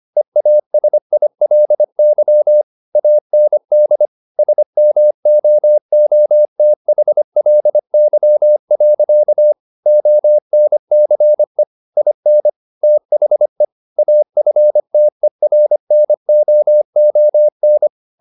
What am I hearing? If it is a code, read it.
Morse code, heard as a keyed tone, about 25 wpm, 600 Hz. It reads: EASILY AND SMOOTHLY. ONCE IN THE AFTERNOON
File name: the old man and the sea part 1 25wpm